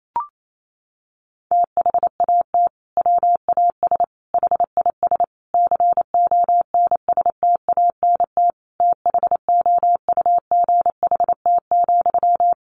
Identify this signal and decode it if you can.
Morse code, heard as a keyed tone, about 28 wpm, 700 Hz. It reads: THAT WAS HIS CONSTANT THOUGHT,
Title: Sentences from Top 1500 Words 28wpm